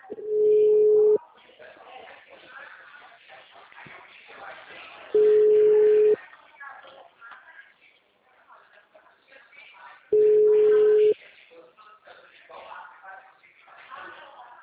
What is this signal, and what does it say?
Ringback tone (Europe, 425 Hz), with speech.